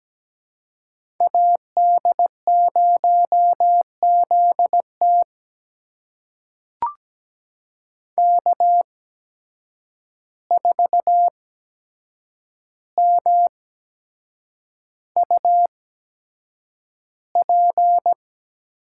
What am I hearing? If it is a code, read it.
Morse code, heard as a keyed tone, about 17 wpm, spaced as for 7 wpm, 700 Hz. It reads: AD0ZT K4MUP
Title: USA Call Signs - 8x Character Spacing - 17wpm